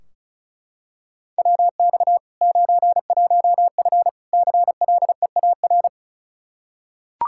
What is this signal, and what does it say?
Morse code, heard as a keyed tone, about 36 wpm, 700 Hz. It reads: WX 91F CLEAR
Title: Practice QSO - Exchange 2 - Info - 35wpm